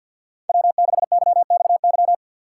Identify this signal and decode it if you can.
Morse code, heard as a keyed tone, about 46 wpm, 700 Hz. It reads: W6YXY
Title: USA Call Signs - 6x Character Spacing - 50wpm